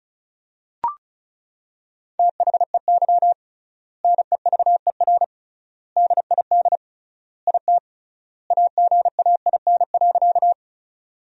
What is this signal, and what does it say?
Morse code, heard as a keyed tone, about 36 wpm, 700 Hz. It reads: THEY NEVER DID IT AGAIN.
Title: 3x Word-Spacing - Sentences from Top 300 Words - 35wpm